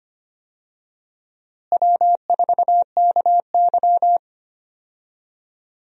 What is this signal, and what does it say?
Morse code, heard as a keyed tone, about 25 wpm, 700 Hz. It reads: W4KY
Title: USA Call Signs - 4 Characters - Repeated 2x - 25wpm